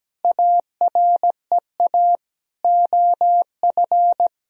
Morse code, heard as a keyed tone, about 17 wpm, 700 Hz. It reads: AREA OF
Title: Sentences from Top 1000 Words - Review Part 2 - 17wpm